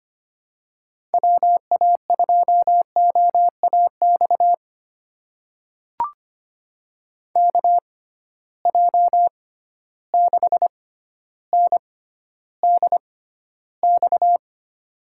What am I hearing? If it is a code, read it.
Morse code, heard as a keyed tone, about 25 wpm, spaced as for 13 wpm, 700 Hz. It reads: WA2OAX KJ6NDX
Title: USA Call Signs - 6 Characters - 6x Character Spacing - 25wpm